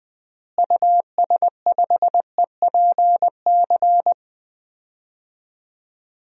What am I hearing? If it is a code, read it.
Morse code, heard as a keyed tone, about 20 wpm, 700 Hz. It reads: US5EPC
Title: DXCC-100 Call Signs - 6 Characters - Repeated 3x - 20wpm